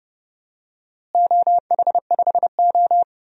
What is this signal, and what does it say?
Morse code, heard as a keyed tone, about 30 wpm, 700 Hz. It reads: OH5O
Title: DXCC-100 Call Signs - 4 Characters - Repeated 2x - 30wpm